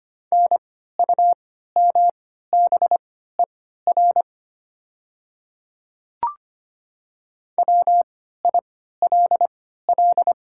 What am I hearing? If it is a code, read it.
Morse code, heard as a keyed tone, about 25 wpm, spaced as for 14 wpm, 700 Hz. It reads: NUMBER WILL
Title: 3x Character Spacing - Top 100 Words - 25wpm